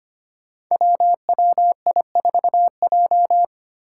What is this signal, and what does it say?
Morse code, heard as a keyed tone, about 25 wpm, 700 Hz. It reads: WWI4J
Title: Five Challenging Letter-Number Combinations 25wpm